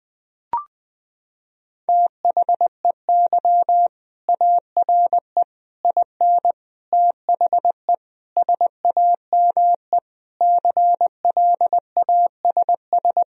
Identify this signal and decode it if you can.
Morse code, heard as a keyed tone, about 21 wpm, 700 Hz. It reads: THEY ARE IN THE SAME CLASS
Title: Sentences from Top 1000 Words - Review Part 2 - 20wpm